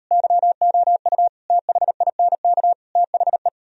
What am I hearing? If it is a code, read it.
Morse code, heard as a keyed tone, about 38 wpm, 700 Hz. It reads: YOU THINK THE